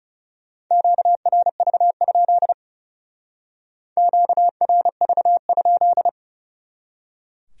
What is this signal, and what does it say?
Morse code, heard as a keyed tone, about 38 wpm, 700 Hz. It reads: QRV? QRV?